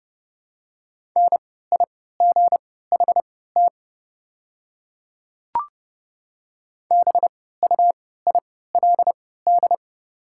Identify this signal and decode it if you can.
Morse code, heard as a keyed tone, about 31 wpm, spaced as for 17 wpm, 700 Hz. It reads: NIGHT BUILD
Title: 3x Character Spacing - Top 500 Words - 30wpm